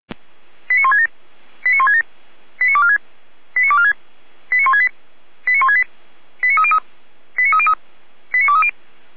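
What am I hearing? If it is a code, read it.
Five-tone selective call (ZVEI), heard as quick runs of five tones, tones of about 70 ms. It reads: 89167 89167 89246 89246 89168 89168 89303 89303 89230
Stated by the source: ZVEI five-tone selective call received on SDR.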